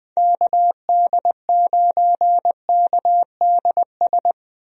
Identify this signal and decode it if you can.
Morse code, heard as a keyed tone, about 21 wpm, 700 Hz. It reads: KD9KDS
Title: USA Call Signs - 6 Characters - 20wpm